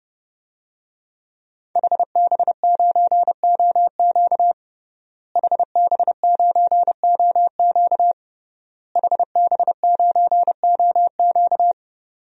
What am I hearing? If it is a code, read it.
Morse code, heard as a keyed tone, about 32 wpm, 700 Hz. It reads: HB9OQ HB9OQ HB9OQ
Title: DXCC-100 Call Signs - 5 Characters - Repeated 3x - 30wpm